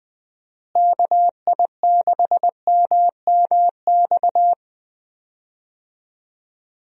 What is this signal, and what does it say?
Morse code, heard as a keyed tone, about 20 wpm, 700 Hz. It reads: KI6MMX
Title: USA Call Signs - 6 Characters - Repeated 3x - 20wpm